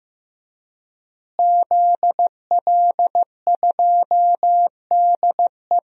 Morse code, heard as a keyed tone, about 15 wpm, 700 Hz. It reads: ZL2DE
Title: DXCC-100 Call Signs - 5 Characters - Repeated 3x - 15wpm